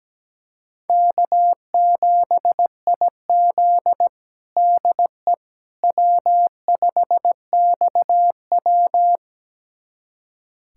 Morse code, heard as a keyed tone, about 17 wpm, 700 Hz. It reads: K7IZ DE W5XW
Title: USA Callsign DE Callsign - 4 Characters - 17wpm